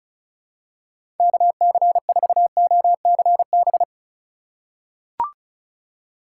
Morse code, heard as a keyed tone, about 36 wpm, 700 Hz. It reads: KC4OCB
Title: USA Call Signs - 2x Character Spacing - 35wpm